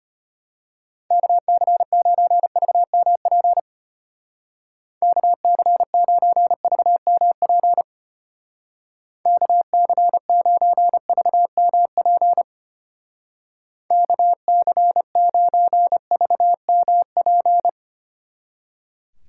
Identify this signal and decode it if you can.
Morse code, heard as a keyed tone, about 32 wpm, 700 Hz. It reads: KC9VMP KC9VMP KC9VMP KC9VMP